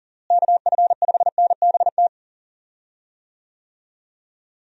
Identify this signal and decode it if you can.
Morse code, heard as a keyed tone, about 39 wpm, 700 Hz. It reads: KF5NBT
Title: USA Call Signs - 6 Characters - Repeated 3x - 40wpm